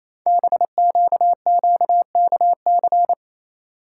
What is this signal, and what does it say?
Morse code, heard as a keyed tone, about 27 wpm, 700 Hz. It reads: BQQKC
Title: Code Groups - 5 Letters - 28wpm